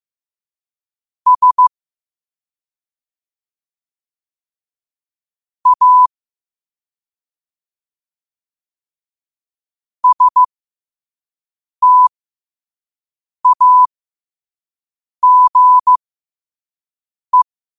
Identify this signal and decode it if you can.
Morse code, heard as a keyed tone, about 15 wpm, spaced as for 5 wpm, 1000 Hz. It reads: S A STAGE
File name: qotdfiles / 5wpm